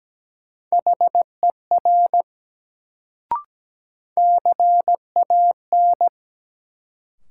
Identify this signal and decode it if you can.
Morse code, heard as a keyed tone, about 17 wpm, 700 Hz. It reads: HER CAN
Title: Sentences from Top 100 Words - New Word Drill - 17wpm